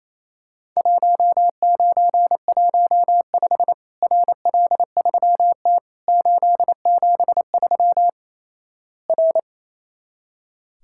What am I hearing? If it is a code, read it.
Morse code, heard as a keyed tone, about 28 wpm, 700 Hz. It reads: 1915 RL3T 873 R
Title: Worked All Europe DX Contest - QTC Exchange - 28wpm